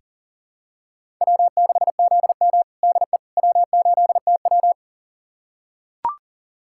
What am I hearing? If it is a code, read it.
Morse code, heard as a keyed tone, about 37 wpm, 700 Hz. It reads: W6ZM DE W8TW
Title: USA Callsign DE Callsign - 4 Characters - 40wpm